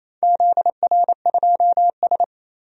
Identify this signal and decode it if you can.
Morse code, heard as a keyed tone, about 28 wpm, 700 Hz. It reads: ZR2S
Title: DXCC-100 Call Signs - 4 Characters - 28wpm